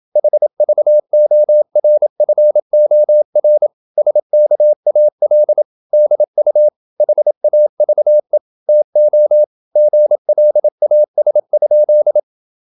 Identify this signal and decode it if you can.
Morse code, heard as a keyed tone, about 28 wpm, 600 Hz. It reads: HVORFOR SKAL DU HAVE TO GLAS?
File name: Ferie QRQ 27wpm